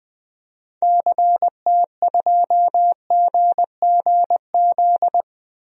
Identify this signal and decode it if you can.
Morse code, heard as a keyed tone, about 20 wpm, 700 Hz. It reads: CT2GGZ
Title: DXCC-100 Call Signs - 6 Characters - 20wpm